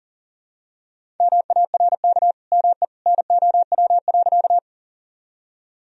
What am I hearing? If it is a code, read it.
Morse code, heard as a keyed tone, about 37 wpm, 700 Hz. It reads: MARK ME NOW.
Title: Sentences from Top 1000 Words - Review Part 1 - 40wpm